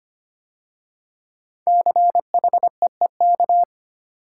Morse code, heard as a keyed tone, about 25 wpm, 700 Hz. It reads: CHEEK
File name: 5 Letter Words - First 3 Letters Given - 25wpm